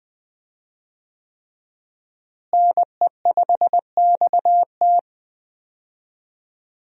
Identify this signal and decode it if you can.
Morse code, heard as a keyed tone, about 20 wpm, 700 Hz. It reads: NE5XT
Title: USA Call Signs - 5 Characters - Repeated 3x - 20wpm